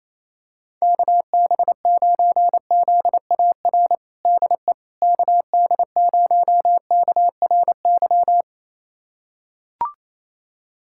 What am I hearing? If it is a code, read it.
Morse code, heard as a keyed tone, about 28 wpm, 700 Hz. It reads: KB9ZAR DE KD0KRY
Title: USA Callsign DE Callsign - 6 Characters - 28wpm